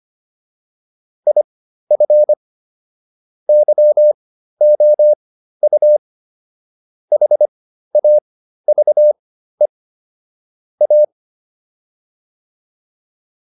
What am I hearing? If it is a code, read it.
Morse code, heard as a keyed tone, about 25 wpm, spaced as for 13 wpm, 600 Hz. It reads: IF YOU HAVE A